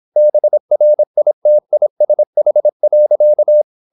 Morse code, heard as a keyed tone, about 26 wpm, 600 Hz. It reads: BRITISH.
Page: Never 26wpm | CW med Gnister